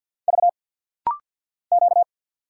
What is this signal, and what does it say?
Morse code, heard as a keyed tone, about 50 wpm, 700 Hz. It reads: V Q
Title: Single Letters - Rapid-Fire 50wpm